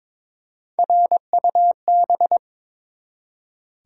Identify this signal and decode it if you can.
Morse code, heard as a keyed tone, about 22 wpm, 700 Hz. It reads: RUB
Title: Three Letter Words - 22wpm